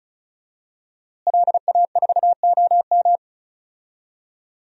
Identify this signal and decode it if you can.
Morse code, heard as a keyed tone, about 36 wpm, 700 Hz. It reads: LA4OM
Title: DXCC-100 Call Signs - 5 Characters - Repeated 2x - 35wpm